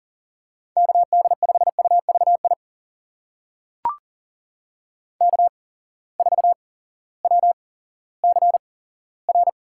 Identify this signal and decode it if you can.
Morse code, heard as a keyed tone, about 40 wpm, spaced as for 18 wpm, 700 Hz. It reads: KD5UVI K4WCR
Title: USA Call Signs - 8x Character Spacing - 40wpm